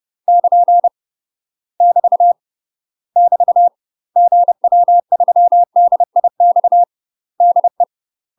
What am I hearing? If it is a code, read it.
Morse code, heard as a keyed tone, about 30 wpm, 700 Hz. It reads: <KN> <BT> <BT> GW3DIX DE
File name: UK MASTER QSOs TRAINING / 30WPM / UK MASTER QSOs 0019